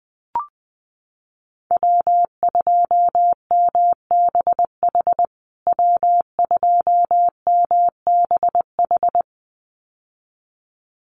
Morse code, heard as a keyed tone, about 20 wpm, 700 Hz. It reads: W2MBH W2MBH
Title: USA Call Signs Repeated 2x 20wpm